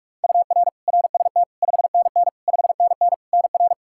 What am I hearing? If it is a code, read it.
Morse code, heard as a keyed tone, about 46 wpm, 700 Hz. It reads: UR RST 5NN 5NN NR
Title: Practice QSO - Exchange 1 - Beginning - 45wpm